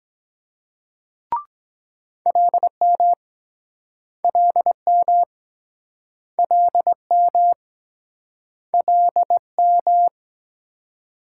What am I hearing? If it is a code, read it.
Morse code, heard as a keyed tone, about 24 wpm, 700 Hz. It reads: LM LM LM LM